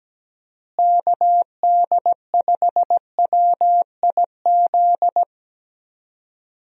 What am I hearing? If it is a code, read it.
Morse code, heard as a keyed tone, about 17 wpm, 700 Hz. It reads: KD5WIZ
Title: USA Call Signs - 6 Characters - Repeated 2x - 17wpm